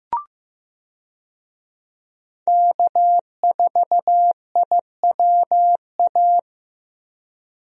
Morse code, heard as a keyed tone, about 15 wpm, 700 Hz. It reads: K4IWA
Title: USA Call Signs - 5 Characters - Repeated 3x - 15wpm